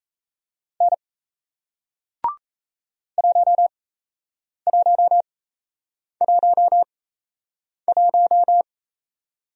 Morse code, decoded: N 1111

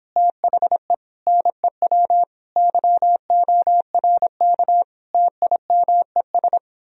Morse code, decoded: THE NEW YORK TIMES

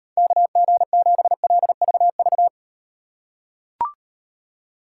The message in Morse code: KG7LVV

